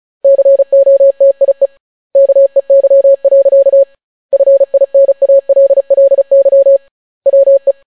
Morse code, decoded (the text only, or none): COTIE KEY. FINALLY WE